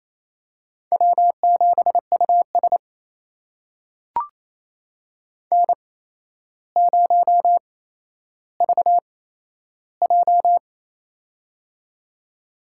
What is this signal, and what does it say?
Morse code, heard as a keyed tone, about 28 wpm, spaced as for 12 wpm, 700 Hz. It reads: W7US N0VJ
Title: USA Call Signs - 4 Characters - 8x Character Spacing - 28wpm